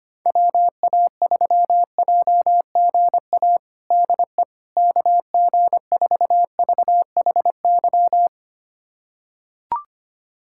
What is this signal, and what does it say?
Morse code, heard as a keyed tone, about 25 wpm, 700 Hz. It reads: WA3JGA DE KG4VHY
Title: USA Callsign DE Callsign 25wpm